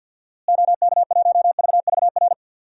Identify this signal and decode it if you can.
Morse code, heard as a keyed tone, about 52 wpm, 700 Hz. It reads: OK1VVR
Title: DXCC-100 Call Signs - 6 Characters - Repeated 3x - 50wpm